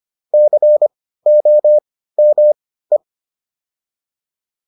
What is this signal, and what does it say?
Morse code, heard as a keyed tone, about 25 wpm, spaced as for 15 wpm, 600 Hz. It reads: COME
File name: WD404 15